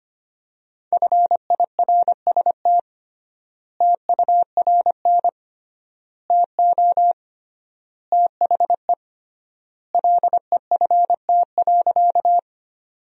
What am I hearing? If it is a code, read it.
Morse code, heard as a keyed tone, about 25 wpm, 700 Hz. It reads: FIRST TURN TO THE LEFT.
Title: 3x Word-Spacing - Sentences from Top 300 Words - 25wpm